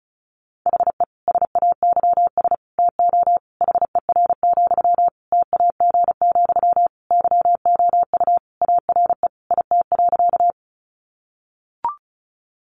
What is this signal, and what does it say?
Morse code, heard as a keyed tone, about 34 wpm, 700 Hz. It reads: HE SAYS TO HER, TAG, YOU ARE IT.
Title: Additional Easy Phrases 35wpm